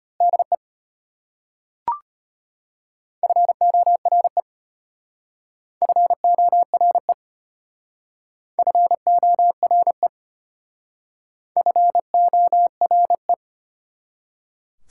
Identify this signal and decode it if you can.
Morse code, heard as a keyed tone, about 32 wpm, 700 Hz. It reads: DE FORE FORE FORE FORE